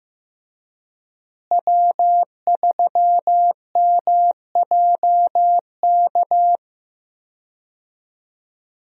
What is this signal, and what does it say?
Morse code, heard as a keyed tone, about 15 wpm, 700 Hz. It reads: W3MJK
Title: USA Call Signs - 5 Characters - Repeated 2x - 15wpm